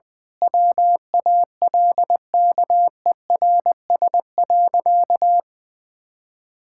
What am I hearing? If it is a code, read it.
Morse code, heard as a keyed tone, about 20 wpm, 700 Hz. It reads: WALKERS.
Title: Pangram Challenge v2 - 20wpm